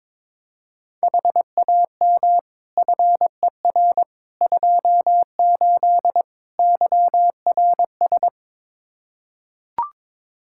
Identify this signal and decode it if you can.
Morse code, heard as a keyed tone, about 21 wpm, 700 Hz. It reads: HAM FER 28 YRS
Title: Practice QSO - Exchange 3 - More Info - 22wpm